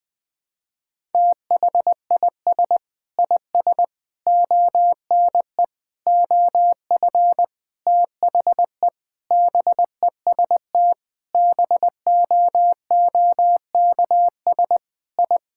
Morse code, heard as a keyed tone, about 20 wpm, 700 Hz. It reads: THIS IS ONE OF THE BEST BOOKS I
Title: Sentences with 5-Word Phrases - 20wpm